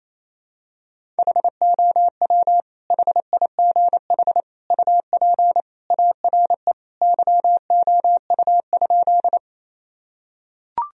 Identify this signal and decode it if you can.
Morse code, heard as a keyed tone, about 28 wpm, 700 Hz. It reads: HOW HIGH UP ARE YOU?
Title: Sentences from Top 1000 Words - Review Part 1 - 28wpm